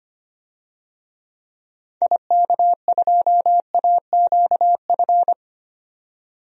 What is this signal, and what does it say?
Morse code, heard as a keyed tone, about 25 wpm, 700 Hz. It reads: IK2AQF
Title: DXCC-100 Call Signs - 6 Characters - 25wpm